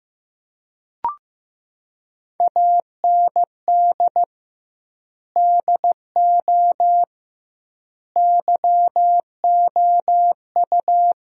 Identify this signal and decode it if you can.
Morse code, heard as a keyed tone, about 15 wpm, 700 Hz. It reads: AND DO YOU
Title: 2x Word-Spacing - Sentences from Top 100 Words - 15wpm